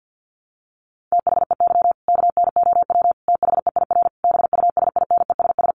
Morse code, heard as a keyed tone, about 50 wpm, 700 Hz. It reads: THEY KNOW THEIR BUSINESS
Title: Sentences from Top 1200 Words 50wpm